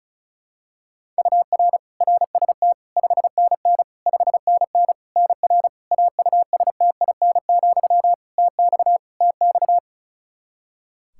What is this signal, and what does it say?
Morse code, heard as a keyed tone, about 36 wpm, 700 Hz. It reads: UR RST 5NN 5NN NR AUSTIN, TX TX
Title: Practice QSO - Exchange 1 - Beginning - 35wpm